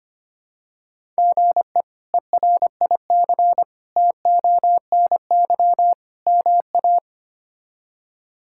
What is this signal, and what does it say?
Morse code, heard as a keyed tone, about 25 wpm, 700 Hz. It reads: GE ERIC TONY MA
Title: K1USN SST QSO Practice - 25wpm